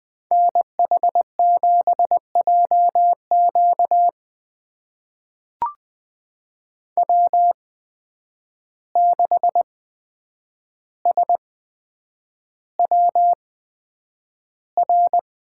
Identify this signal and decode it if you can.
Morse code, heard as a keyed tone, about 20 wpm, spaced as for 9 wpm, 700 Hz. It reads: NH7JQ W6SWR
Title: USA Call Signs - 5 Characters - 8x Character Spacing - 20wpm